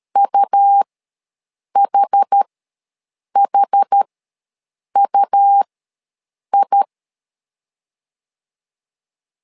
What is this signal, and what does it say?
Morse code, heard as a keyed tone, about 13 wpm, spaced as for 7 wpm, 800 Hz. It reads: UHHUI
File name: LEZIONI / 2 UHDN / 2 UH ws 12wpm 1730